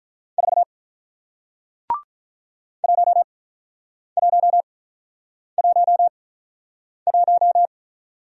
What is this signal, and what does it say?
Morse code, heard as a keyed tone, about 48 wpm, spaced as for 9 wpm, 700 Hz. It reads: 4 1111